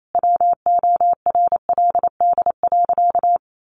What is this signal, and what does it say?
Morse code, heard as a keyed tone, about 28 wpm, 700 Hz. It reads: WORLD.
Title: Sentences from Top 700 Words 28wpm